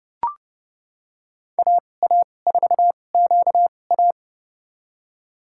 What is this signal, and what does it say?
Morse code, heard as a keyed tone, about 30 wpm, spaced as for 22 wpm, 700 Hz. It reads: AA4QA
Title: USA Call Signs - 2x Character Spacing - 30wpm